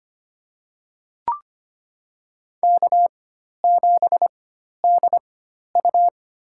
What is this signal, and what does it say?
Morse code, heard as a keyed tone, about 25 wpm, spaced as for 12 wpm, 700 Hz. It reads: K7DU
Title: USA Call Signs - 4 Characters - 4x Character Spacing - 25wpm